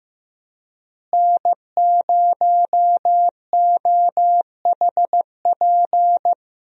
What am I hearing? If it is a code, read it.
Morse code, heard as a keyed tone, about 15 wpm, 700 Hz. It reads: N0OHP
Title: USA Call Signs - 5 Characters - Repeated 3x - 15wpm